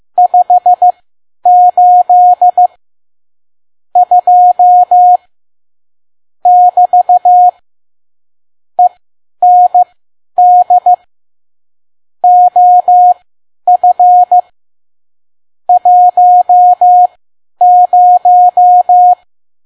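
Morse code, heard as a keyed tone, about 15 wpm, spaced as for 10 wpm, 720 Hz. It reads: 58 2 <BT> END OF 10
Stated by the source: Morse code practice at 10 WPM.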